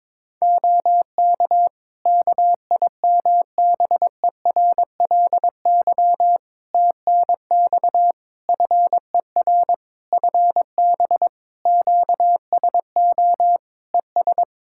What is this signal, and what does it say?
Morse code, heard as a keyed tone, about 22 wpm, 700 Hz. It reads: OK KIMBERLY TNX FER FB QSO ES
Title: Practice QSO - Exchange 4 - Ending - 22wpm